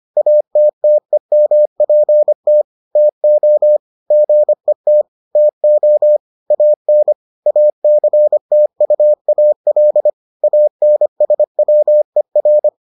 Morse code, heard as a keyed tone, about 25 wpm, 600 Hz. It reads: ATTEMPT TO GET TO AN ACTUAL ANSWER